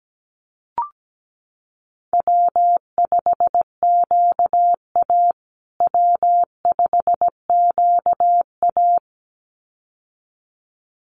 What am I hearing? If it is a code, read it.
Morse code, heard as a keyed tone, about 17 wpm, 700 Hz. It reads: W5QA W5QA
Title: USA Call Signs Repeated 2x 17wpm